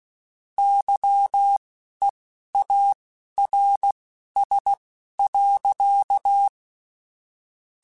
Morse code, heard as a keyed tone, about 16 wpm, spaced as for 11 wpm, 800 Hz. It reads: YEARS.
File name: QSOs / QSOs-mp3s / kc7sac@10